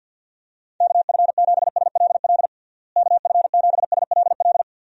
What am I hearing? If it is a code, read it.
Morse code, heard as a keyed tone, about 50 wpm, 700 Hz. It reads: KF7SLL KF7SLL
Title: USA Call Signs - 6 Characters - Repeated 2x - 50wpm